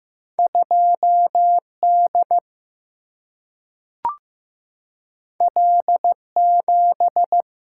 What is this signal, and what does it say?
Morse code, heard as a keyed tone, about 15 wpm, 700 Hz. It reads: 2D L7
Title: Single-Letter Single-Number 15wpm